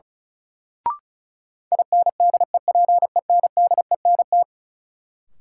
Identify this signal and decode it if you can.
Morse code, heard as a keyed tone, about 36 wpm, 700 Hz. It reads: INDEPENDENT
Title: Sentences from Top 900 Words - New Word Drill - 35wpm